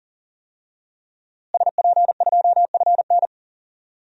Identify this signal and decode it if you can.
Morse code, heard as a keyed tone, about 40 wpm, 700 Hz. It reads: SP2FN
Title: DXCC-100 Call Signs - 5 Characters - Repeated 2x - 40wpm